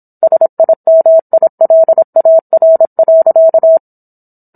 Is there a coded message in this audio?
Yes: Morse code from 0.2 to 3.8 s, heard as a keyed tone, about 27 wpm, 650 Hz. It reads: SIMILAR.